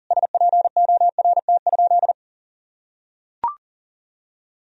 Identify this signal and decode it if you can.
Morse code, heard as a keyed tone, about 40 wpm, 700 Hz. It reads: SPORT?
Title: Sentences from Top 1000 Words - Review Part 4 - 40wpm